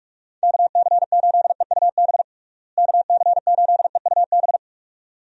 Morse code, heard as a keyed tone, about 42 wpm, 700 Hz. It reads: KC8EUB KC8EUB
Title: USA Call Signs - 6 Characters - Repeated 2x - 45wpm